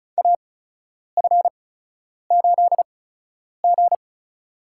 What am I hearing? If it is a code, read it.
Morse code, heard as a keyed tone, about 34 wpm, spaced as for 10 wpm, 700 Hz. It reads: AF8G
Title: USA Call Signs - 4 Characters - 8x Character Spacing - 35wpm